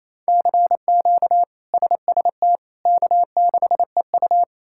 Morse code, heard as a keyed tone, about 28 wpm, 700 Hz. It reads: CQ SST K6EU
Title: K1USN SST QSO Practice - 28wpm